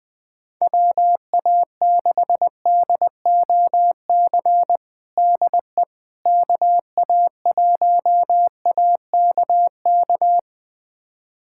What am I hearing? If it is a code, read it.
Morse code, heard as a keyed tone, about 20 wpm, 700 Hz. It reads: WA6DOC DE KA1AKK
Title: USA Callsign DE Callsign - 6 Characters - 20wpm